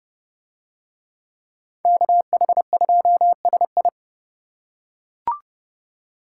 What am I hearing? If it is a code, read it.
Morse code, heard as a keyed tone, about 29 wpm, 700 Hz. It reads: KH2SI